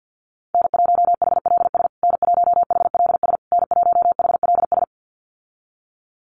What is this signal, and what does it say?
Morse code, heard as a keyed tone, about 48 wpm, 700 Hz. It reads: N1HLS N1HLS N1HLS
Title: USA Call Signs Repeated 3x 50wpm